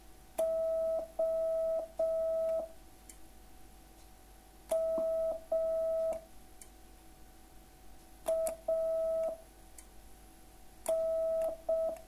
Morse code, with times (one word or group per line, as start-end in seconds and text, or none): 0.4-2.6 O
4.7-6.1 M
8.3-9.3 A
10.9-11.9 N